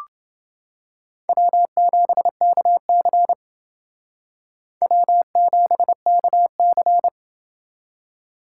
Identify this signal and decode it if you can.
Morse code, heard as a keyed tone, about 29 wpm, 700 Hz. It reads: W7KC W7KC